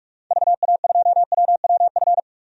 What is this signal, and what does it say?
Morse code, heard as a keyed tone, about 46 wpm, 700 Hz. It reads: VA2WWF
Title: DXCC-100 Call Signs - 6 Characters - 45wpm